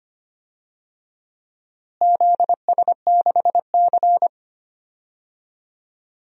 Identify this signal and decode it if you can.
Morse code, heard as a keyed tone, about 25 wpm, 700 Hz. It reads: ZS6C
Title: DXCC-100 Call Signs - 4 Characters - Repeated 3x - 25wpm